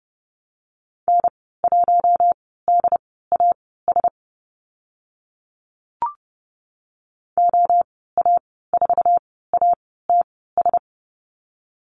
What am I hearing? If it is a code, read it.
Morse code, heard as a keyed tone, about 30 wpm, spaced as for 17 wpm, 700 Hz. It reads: N1DAS OA4ATS
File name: Lesson 15g - Number 5 - Call Signs 30wpm